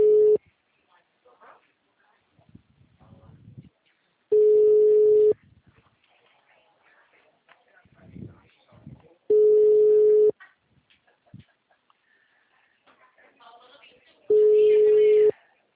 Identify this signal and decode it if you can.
Ringback tone (Europe, 425 Hz), with speech.
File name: out- -2033- - - .42989